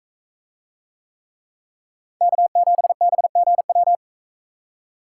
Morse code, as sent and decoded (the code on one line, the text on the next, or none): -.- --... -... --. .--
K7BGW